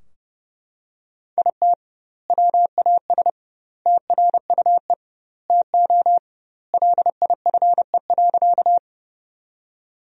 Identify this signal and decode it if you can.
Morse code, heard as a keyed tone, about 30 wpm, 700 Hz. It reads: IT WAS TRUE TO LIFE.
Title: 2x Word-Spacing - Sentences from Top 400 Words - 30wpm